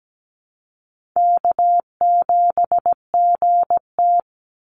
Morse code, heard as a keyed tone, about 17 wpm, 700 Hz. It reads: K7GT
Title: Call Signs Repeated 2x - 17wpm